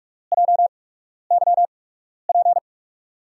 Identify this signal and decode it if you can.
Morse code, heard as a keyed tone, about 44 wpm, spaced as for 12 wpm, 700 Hz. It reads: JYP
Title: USA Call Signs - 8x Character Spacing - 45wpm